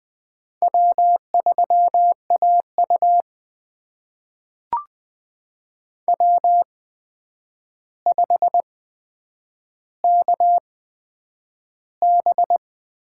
Morse code, heard as a keyed tone, about 20 wpm, spaced as for 9 wpm, 700 Hz. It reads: W3AU W5KB
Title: USA Call Signs - 4 Characters - 8x Character Spacing - 20wpm